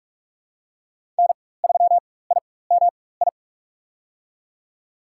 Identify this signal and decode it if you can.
Morse code, heard as a keyed tone, about 46 wpm, spaced as for 21 wpm, 700 Hz. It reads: N3IMI
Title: USA Call Signs - 5 Characters - 4x Character Spacing - 45wpm